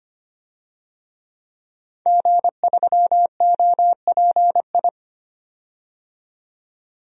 Morse code, heard as a keyed tone, about 25 wpm, 700 Hz. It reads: G3OPI
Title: DXCC-100 Call Signs - 5 Characters - 25wpm